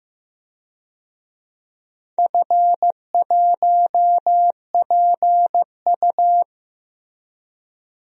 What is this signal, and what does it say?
Morse code, heard as a keyed tone, about 15 wpm, 700 Hz. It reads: F1PU
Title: DXCC-100 Call Signs - 4 Characters - Repeated 3x - 15wpm